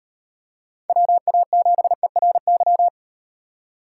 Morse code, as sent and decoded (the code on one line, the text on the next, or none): .-- .- --... .-. -.--
WA7ERY